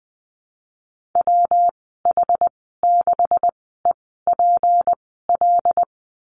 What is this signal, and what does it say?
Morse code, heard as a keyed tone, about 20 wpm, spaced as for 14 wpm, 700 Hz. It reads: WH6EPL
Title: 193 Lesson 31g - Letter v - Call Signs 20wpm